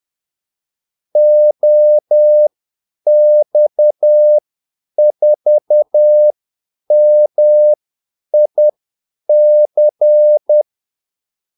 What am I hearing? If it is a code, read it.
Morse code, heard as a keyed tone, about 10 wpm, spaced as for 8 wpm, 600 Hz. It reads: OX4MIC
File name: Callsigns-10-8wpm